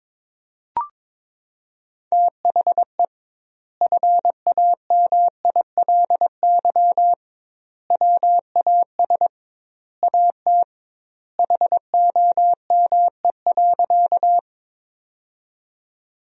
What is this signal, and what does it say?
Morse code, heard as a keyed tone, about 22 wpm, 700 Hz. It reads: THE FAMILY WAS AT HOME.
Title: 2x Word-Spacing - Sentences from Top 400 Words - 22wpm